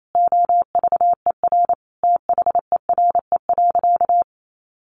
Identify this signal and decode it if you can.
Morse code, heard as a keyed tone, about 28 wpm, 700 Hz. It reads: OVER THERE.
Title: Sentences from Top 1100 Words 28wpm